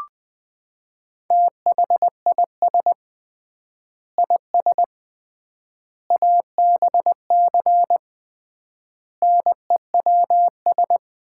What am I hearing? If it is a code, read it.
Morse code, heard as a keyed tone, about 20 wpm, 700 Hz. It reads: THIS IS ABC NEWS
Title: 3x Word-Spacing - Sets of 5 Words - 20wpm